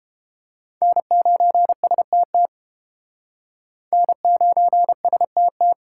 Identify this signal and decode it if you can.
Morse code, heard as a keyed tone, about 32 wpm, 700 Hz. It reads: N9STT N9STT